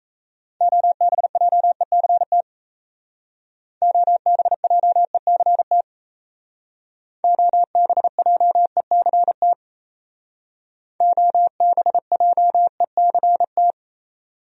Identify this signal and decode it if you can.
Morse code, heard as a keyed tone, about 33 wpm, 700 Hz. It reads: OBJECT OBJECT OBJECT OBJECT
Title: Speed-Racing Top 500 Words 28wpm